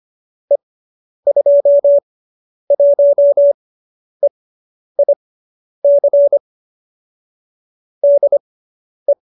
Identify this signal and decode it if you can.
Morse code, heard as a keyed tone, about 25 wpm, spaced as for 10 wpm, 570 Hz. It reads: E21EIC DE